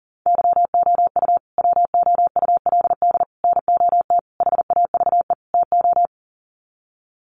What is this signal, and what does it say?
Morse code, heard as a keyed tone, about 40 wpm, 700 Hz. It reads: YOU WOULD NOT HAVE TO
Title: Sets of 5 Words Encore 40wpm